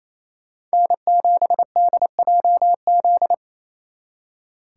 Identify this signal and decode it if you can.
Morse code, heard as a keyed tone, about 36 wpm, 700 Hz. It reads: N7DJZ